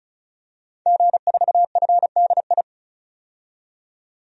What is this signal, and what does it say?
Morse code, heard as a keyed tone, about 36 wpm, 700 Hz. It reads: G4FDI
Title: DXCC-100 Call Signs - 5 Characters - 35wpm